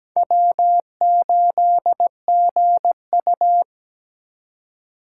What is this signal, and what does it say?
Morse code, heard as a keyed tone, about 17 wpm, 700 Hz. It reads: W8GU
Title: K1USN SST QSO Practice - 17wpm